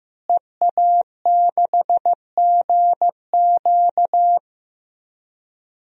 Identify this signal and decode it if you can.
Morse code, heard as a keyed tone, about 15 wpm, 700 Hz. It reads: EA6GQ
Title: DXCC-100 Call Signs - 5 Characters - Repeated 2x - 15wpm